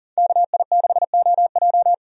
Morse code, heard as a keyed tone, about 40 wpm, 700 Hz. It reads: KI6OJ
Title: USA Call Signs - 5 Characters - 40wpm